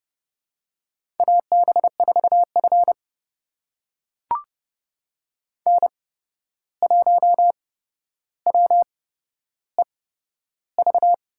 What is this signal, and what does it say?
Morse code, heard as a keyed tone, about 32 wpm, spaced as for 12 wpm, 700 Hz. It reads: AB4F N1WEV
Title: USA Call Signs - 8x Character Spacing - 30wpm